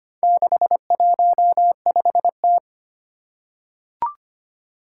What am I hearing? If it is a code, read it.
Morse code, heard as a keyed tone, about 25 wpm, 700 Hz. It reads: 615T
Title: Four Challenging Letter-Number Combinations 25wpm